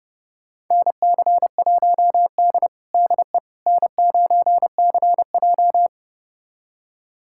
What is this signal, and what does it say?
Morse code, heard as a keyed tone, about 30 wpm, 700 Hz. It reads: NC1D DE N9CJ
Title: USA Callsign DE Callsign - 4 Characters - 30wpm